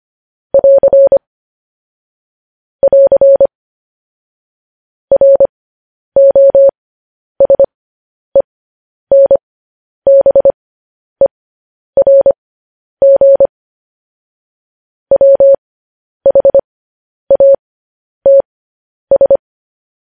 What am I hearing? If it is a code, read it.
Morse code, heard as a keyed tone, about 25 wpm, spaced as for 10 wpm, 550 Hz. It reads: <AR> <AR> ROSENBERG WHATS
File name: podcast / News headlines-1x-25x10